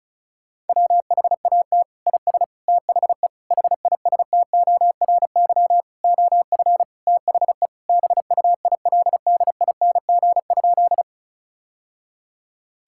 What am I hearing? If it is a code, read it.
Morse code, heard as a keyed tone, about 36 wpm, 700 Hz. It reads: WHAT IS THE HISTORY OF THE BUILDING?
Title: Sentences from Top 1000 Words - Review Part 5 - 35wpm